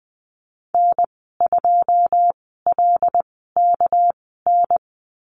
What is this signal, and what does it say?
Morse code, heard as a keyed tone, about 20 wpm, spaced as for 14 wpm, 700 Hz. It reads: N2LKN
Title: 200 Lesson 32g - Letter k - Call Signs 20wpm